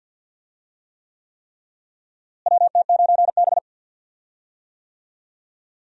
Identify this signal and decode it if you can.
Morse code, heard as a keyed tone, about 51 wpm, 700 Hz. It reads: WT9B
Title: USA Call Signs - 4 Characters - Repeated 3x - 50wpm